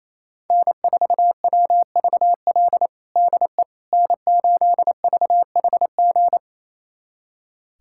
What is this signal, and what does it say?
Morse code, heard as a keyed tone, about 28 wpm, 700 Hz. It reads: N4WVL DE N8VHG
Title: USA Callsign DE Callsign - 5 Characters - 28wpm